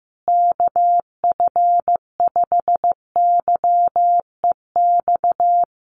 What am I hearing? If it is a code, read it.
Morse code, heard as a keyed tone, about 15 wpm, 700 Hz. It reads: KF5YEX
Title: USA Call Signs Repeated 3x 15wpm